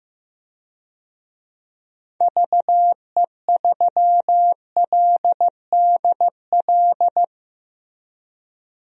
Morse code, heard as a keyed tone, about 15 wpm, 700 Hz. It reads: VE3LDL
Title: DXCC-100 Call Signs - 6 Characters - 15wpm